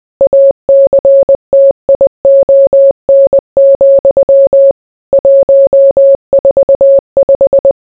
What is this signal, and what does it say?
Morse code, heard as a keyed tone, about 20 wpm, 550 Hz. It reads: ACTION, 145